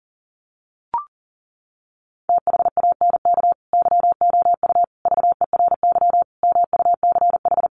Morse code, heard as a keyed tone, about 40 wpm, 700 Hz. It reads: THANK YOU VERY MUCH
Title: Sets of 5 Words 40wpm